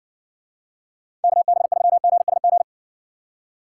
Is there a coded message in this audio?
Yes: Morse code from 1.2 to 2.6 s, heard as a keyed tone, about 60 wpm, 700 Hz. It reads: KB3GSG